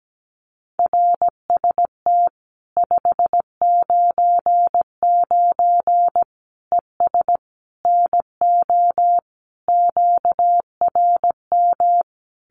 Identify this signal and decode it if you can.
Morse code, heard as a keyed tone, about 17 wpm, 700 Hz. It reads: RST 599 ES NO QRM
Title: Common QSO Phrases 17wpm